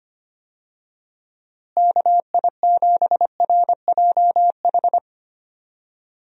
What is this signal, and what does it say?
Morse code, heard as a keyed tone, about 25 wpm, 700 Hz. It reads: KI7RJH
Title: USA Call Signs - 6 Characters - Repeated 3x - 25wpm